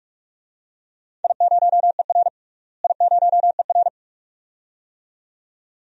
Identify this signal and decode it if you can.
Morse code, heard as a keyed tone, about 41 wpm, 700 Hz. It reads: I0ER I0ER
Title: DXCC-100 Call Signs - 4 Characters - Repeated 2x - 45wpm